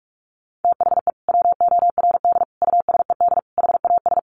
Morse code, heard as a keyed tone, about 46 wpm, 700 Hz. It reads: THE WORD USED HAS
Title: Sentences from Top 1100 Words 45wpm